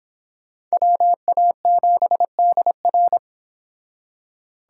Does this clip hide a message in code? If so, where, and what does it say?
Yes: Morse code from 0.7 to 3.2 s, heard as a keyed tone, about 30 wpm, 700 Hz. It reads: WA7DR